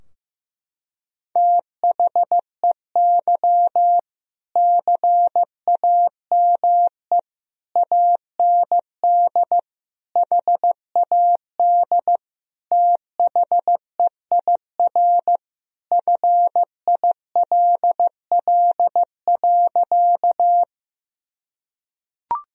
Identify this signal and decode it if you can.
Morse code, heard as a keyed tone, about 15 wpm, 700 Hz. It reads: THEY CAME AND HAD THEIR FILL.
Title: Sentences from Top 1000 Words - Review Part 3 - 15wpm